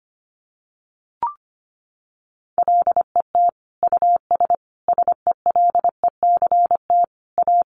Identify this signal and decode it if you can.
Morse code, heard as a keyed tone, about 25 wpm, 700 Hz. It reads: LET US SELECT A
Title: Sentences from Top 1300 Words 25wpm